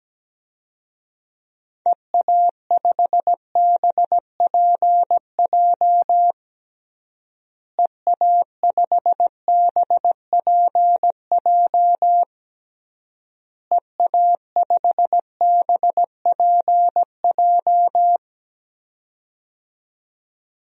Morse code, heard as a keyed tone, about 17 wpm, 700 Hz. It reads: EA5BPJ EA5BPJ EA5BPJ
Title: DXCC-100 Call Signs - 6 Characters - Repeated 3x - 17wpm